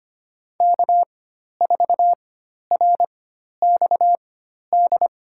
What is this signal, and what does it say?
Morse code, heard as a keyed tone, about 25 wpm, spaced as for 12 wpm, 700 Hz. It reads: K4RXD
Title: USA Call Signs - 5 Characters - 4x Character Spacing - 25wpm